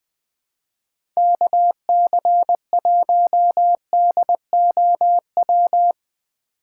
Morse code, heard as a keyed tone, about 20 wpm, 700 Hz. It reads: KC1DOW
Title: USA Call Signs - 6 Characters - 6x Character Spacing - 20wpm